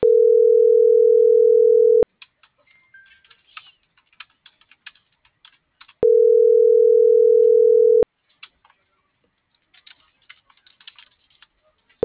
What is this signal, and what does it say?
Ringback tone (North America), 440 + 480 Hz.
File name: CENTRAL MORENA / out jacque.aguilera.25 - .27423